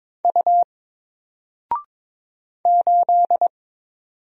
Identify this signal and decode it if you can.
Morse code, heard as a keyed tone, about 22 wpm, 700 Hz. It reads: U 8
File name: Single-Letter-Number-22wpm